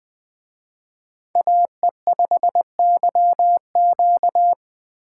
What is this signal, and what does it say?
Morse code, heard as a keyed tone, about 20 wpm, 700 Hz. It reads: AE5YQ
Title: Speed Racing - USA 2x2 Call Signs 17wpm